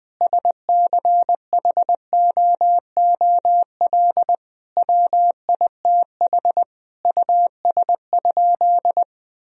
Morse code, heard as a keyed tone, about 20 wpm, 700 Hz. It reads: SCHOOL WITH US?
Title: Sentences from Top 1000 Words - Review Part 3 - 20wpm